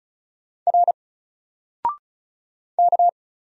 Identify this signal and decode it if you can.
Morse code, heard as a keyed tone, about 35 wpm, 700 Hz. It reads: R K
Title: Single Letter-Number 35wpm